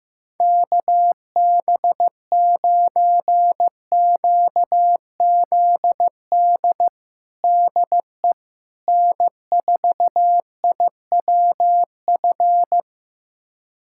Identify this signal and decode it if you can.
Morse code, heard as a keyed tone, about 16 wpm, 700 Hz. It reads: KB9QZD DE N4IWF
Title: USA Callsign DE Callsign 15wpm